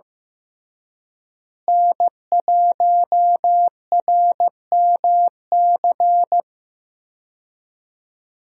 Morse code, heard as a keyed tone, about 16 wpm, 700 Hz. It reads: N1RMC
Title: USA Call Signs - 5 Characters - Repeated 3x - 15wpm